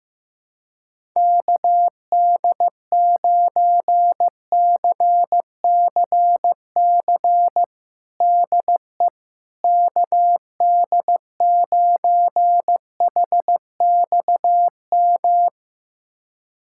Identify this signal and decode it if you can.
Morse code, heard as a keyed tone, about 15 wpm, 700 Hz. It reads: KD9CCC DE KD9HXM
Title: USA Callsign DE Callsign - 6 Characters - 15wpm